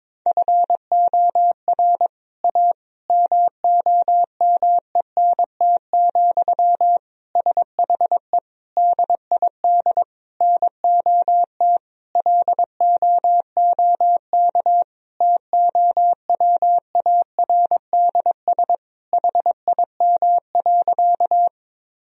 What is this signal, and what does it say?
Morse code, heard as a keyed tone, about 22 wpm, 700 Hz. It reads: FOR A MOMENT, SHE DID NOT LOOK TOWARDS HIM.
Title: Sentences from Top 1000 Words - Review Part 2 - 22wpm